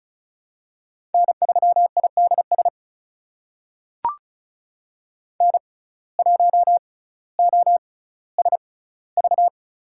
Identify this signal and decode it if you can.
Morse code, heard as a keyed tone, about 36 wpm, spaced as for 18 wpm, 700 Hz. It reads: N3IDS N1OSV
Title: USA Call Signs - 5 Characters - 6x Character Spacing - 35wpm